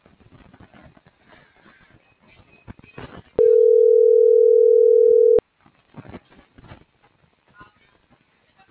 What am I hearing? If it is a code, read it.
Ringback tone (North America), 440 + 480 Hz.